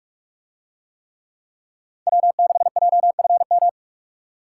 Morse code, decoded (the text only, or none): W6JFM